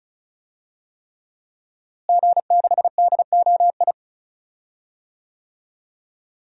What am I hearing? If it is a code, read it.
Morse code, heard as a keyed tone, about 34 wpm, 700 Hz. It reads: G6DOI
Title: DXCC-100 Call Signs - 5 Characters - Repeated 3x - 35wpm